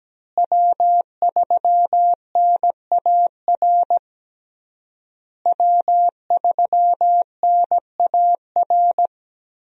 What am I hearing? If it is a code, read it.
Morse code, heard as a keyed tone, about 17 wpm, 700 Hz. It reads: W3NAR W3NAR
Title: USA Call Signs - 5 Characters - Repeated 3x - 17wpm